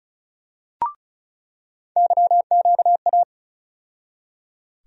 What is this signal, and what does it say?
Morse code, heard as a keyed tone, about 36 wpm, 700 Hz. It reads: YQA
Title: USA Call Sign Postfixes - 3 Letters 35wpm